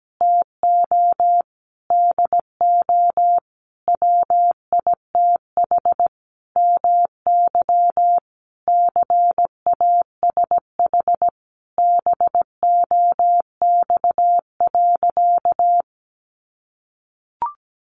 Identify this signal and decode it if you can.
Morse code, heard as a keyed tone, about 17 wpm, 700 Hz. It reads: TO DO WITH MY CASH BOX.
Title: Sentences from Top 700 Words 17wpm